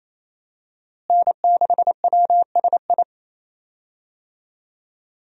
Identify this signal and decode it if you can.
Morse code, heard as a keyed tone, about 28 wpm, 700 Hz. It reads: N6WSI
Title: USA Call Signs - 5 Characters - Repeated 2x - 28wpm